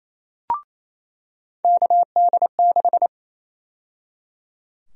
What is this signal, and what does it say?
Morse code, heard as a keyed tone, about 28 wpm, 700 Hz. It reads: KD6
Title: USA Call Sign Prefixes - 28wpm